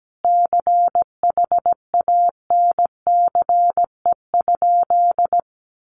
Morse code, heard as a keyed tone, about 17 wpm, 700 Hz. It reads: CHANCE?
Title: Sentences from Top 800 Words 17wpm